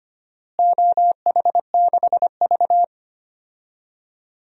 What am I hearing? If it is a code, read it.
Morse code, heard as a keyed tone, about 25 wpm, 700 Hz. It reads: OH6V
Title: DXCC-100 Call Signs - 4 Characters - Repeated 2x - 25wpm